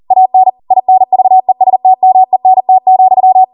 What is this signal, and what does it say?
Morse code, heard as a keyed tone, about 40 wpm, 750 Hz. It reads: AN INVESTMENT,
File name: Morse / 40WPM